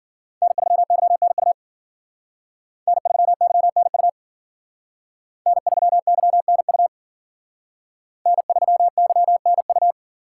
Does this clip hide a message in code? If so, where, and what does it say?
Yes: Morse code from 0.4 to 9.9 s, heard as a keyed tone, about 51 wpm, 700 Hz. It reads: N3YNU N3YNU N3YNU N3YNU